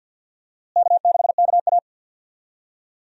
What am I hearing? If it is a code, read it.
Morse code, heard as a keyed tone, about 50 wpm, 700 Hz. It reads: K6KA
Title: USA Call Signs - 4 Characters - Repeated 2x - 50wpm